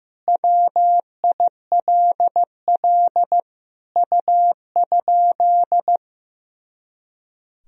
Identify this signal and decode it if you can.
Morse code, heard as a keyed tone, about 16 wpm, 700 Hz. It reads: WILL U?
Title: Sentences from Top 300 Words and Abbreviations 15wpm